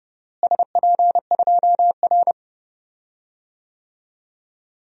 Morse code, heard as a keyed tone, about 30 wpm, 700 Hz. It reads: SP2R
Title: DXCC-100 Call Signs - 4 Characters - Repeated 3x - 30wpm